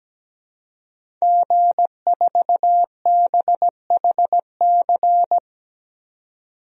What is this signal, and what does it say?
Morse code, heard as a keyed tone, about 17 wpm, 700 Hz. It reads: G4BHC
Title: DXCC-100 Call Signs - 5 Characters - 17wpm